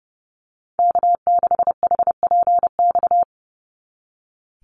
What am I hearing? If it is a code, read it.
Morse code, heard as a keyed tone, about 30 wpm, 700 Hz. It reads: K6HPX
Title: NA SOTA Activator Callsigns 30wpm